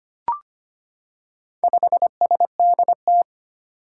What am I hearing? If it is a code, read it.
Morse code, heard as a keyed tone, about 25 wpm, 700 Hz. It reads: 5SDT
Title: Four Challenging Letter-Number Combinations 25wpm